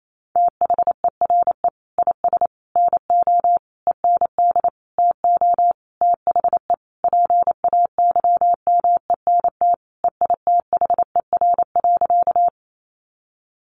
Morse code, heard as a keyed tone, about 28 wpm, 700 Hz. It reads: THERE IS NO END TO THE PAYMENT EITHER.
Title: Sentences from Top 1200 Words 28wpm